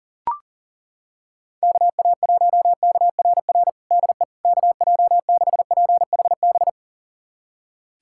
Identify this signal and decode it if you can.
Morse code, heard as a keyed tone, about 37 wpm, 700 Hz. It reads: KA1KRR DE KJ6PHB
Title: USA Callsign DE Callsign - 6 Characters - 40wpm